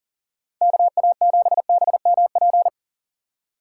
Morse code, heard as a keyed tone, about 37 wpm, 700 Hz. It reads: KA7BMP
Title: USA Call Signs - 6 Characters - 6x Character Spacing - 40wpm